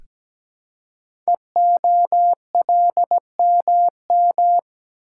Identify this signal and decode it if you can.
Morse code, heard as a keyed tone, about 17 wpm, 700 Hz. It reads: EOLMM
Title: Code Groups - 5 Letters - 17wpm